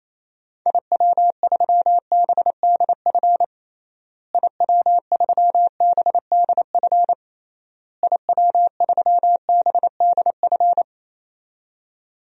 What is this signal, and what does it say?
Morse code, heard as a keyed tone, about 28 wpm, 700 Hz. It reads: IW3BDF IW3BDF IW3BDF
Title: DXCC-100 Call Signs - 6 Characters - Repeated 3x - 28wpm